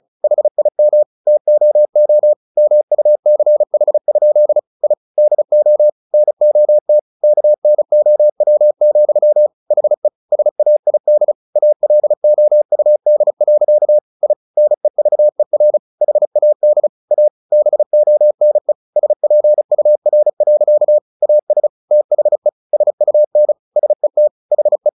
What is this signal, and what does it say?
Morse code, heard as a keyed tone, about 34 wpm, 600 Hz. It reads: HIM TOO MUCH? I DO NOT KNOW, HE SAID ALOUD. I NEVER HAD A BONE SPUR. AS THE SUN SET HE
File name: the old man and the sea / the old man and the sea 1 35wpm